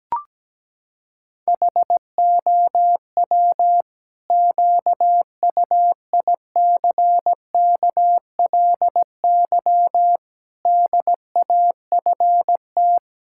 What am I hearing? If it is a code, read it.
Morse code, heard as a keyed tone, about 17 wpm, 700 Hz. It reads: HOW QUICKLY DAFT
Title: Pangram Challenge v2 - 17wpm